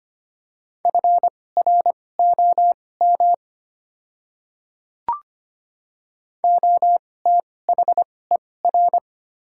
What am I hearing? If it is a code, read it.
Morse code, heard as a keyed tone, about 25 wpm, spaced as for 18 wpm, 700 Hz. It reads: FROM OTHER
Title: 2x Character Spacing - Top 200 Words - 25wpm